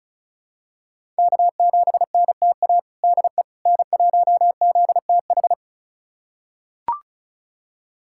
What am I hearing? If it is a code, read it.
Morse code, heard as a keyed tone, about 34 wpm, 700 Hz. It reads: K7NTA DE N1ZTH
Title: USA Callsign DE Callsign - 5 Characters - 35wpm